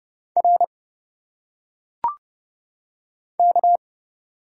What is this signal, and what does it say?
Morse code, heard as a keyed tone, about 30 wpm, 700 Hz. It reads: R K
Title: Single Letters - Phonetic Pronunciation 30wpm